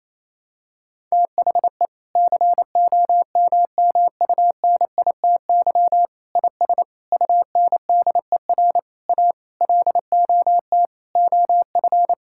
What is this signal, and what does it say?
Morse code, heard as a keyed tone, about 28 wpm, 700 Hz. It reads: THE COMMUNITY IS UNDER A LOT OF
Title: Sentences from Top 1000 Words - Review Part 6 - 28wpm